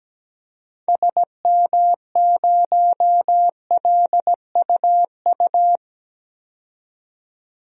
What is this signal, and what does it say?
Morse code, heard as a keyed tone, about 17 wpm, 700 Hz. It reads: SM0LUU
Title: DXCC-100 Call Signs - 6 Characters - 17wpm